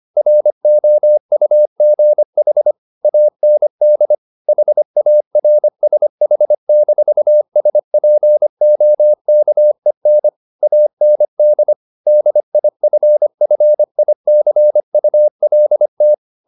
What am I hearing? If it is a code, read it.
Morse code, heard as a keyed tone, about 25 wpm, 600 Hz. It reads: ROUGH AND HARSH-SPOKEN AND DIFFICULT
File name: the old man and the sea part 1 25wpm